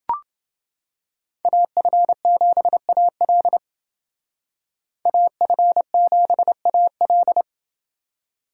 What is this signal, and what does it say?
Morse code, heard as a keyed tone, about 28 wpm, 700 Hz. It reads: AF7AL AF7AL